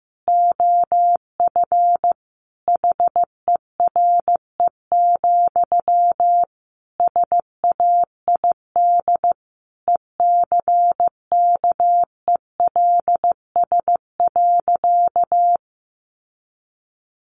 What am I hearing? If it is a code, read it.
Morse code, heard as a keyed tone, about 15 wpm, 700 Hz. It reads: OF HERE, SAID ECKELS.
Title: A Sound Of Thunder 15wpm